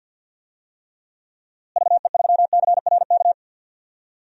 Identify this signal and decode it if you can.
Morse code, heard as a keyed tone, about 52 wpm, 700 Hz. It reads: VE3CRK
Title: DXCC-100 Call Signs - 6 Characters - Repeated 3x - 50wpm